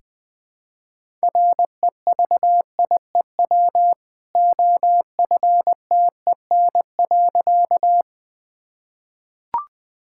Morse code, heard as a keyed tone, about 21 wpm, 700 Hz. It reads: REVIEW OFTEN.
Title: Easy Sentences with Hints - 20wpm